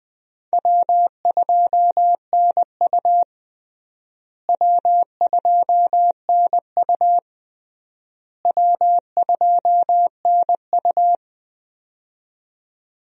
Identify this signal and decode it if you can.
Morse code, heard as a keyed tone, about 21 wpm, 700 Hz. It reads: W2NU W2NU W2NU
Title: USA Call Signs - 4 Characters - Repeated 3x - 20wpm